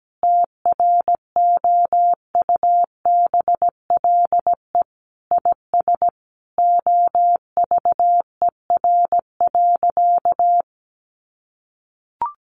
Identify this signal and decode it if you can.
Morse code, heard as a keyed tone, about 17 wpm, 700 Hz. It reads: TROUBLE IS OVER.
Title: Sentences from Top 1200 Words 17wpm